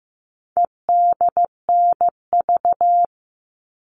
Morse code, heard as a keyed tone, about 15 wpm, 700 Hz. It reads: EDNV
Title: All Cut Numbers - 4-Digit 15wpm